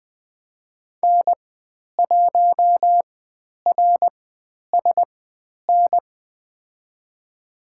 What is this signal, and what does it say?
Morse code, heard as a keyed tone, about 20 wpm, spaced as for 10 wpm, 700 Hz. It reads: N1RSN
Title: 109 Lesson 18g - Letter m - Call Signs 20wpm